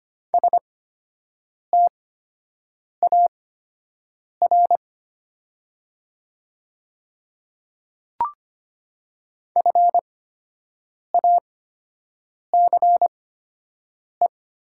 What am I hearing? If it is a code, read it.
Morse code, heard as a keyed tone, about 25 wpm, spaced as for 7 wpm, 700 Hz. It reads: STAR FACE
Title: 8x Character Spacing - Top 500 Words - 25wpm